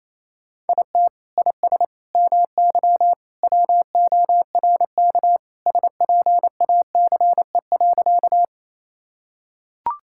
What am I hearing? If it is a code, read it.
Morse code, heard as a keyed tone, about 28 wpm, 700 Hz. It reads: IT IS MY WORK SPACE.
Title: Sentences from Top 1000 Words - Review Part 5 - 28wpm